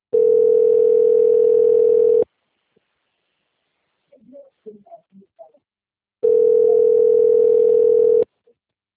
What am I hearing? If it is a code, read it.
Ringback tone (North America), 440 + 480 Hz.